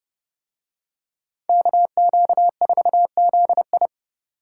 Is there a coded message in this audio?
Yes: Morse code from 1.5 to 3.9 s, heard as a keyed tone, about 29 wpm, 700 Hz. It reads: KQ4ZI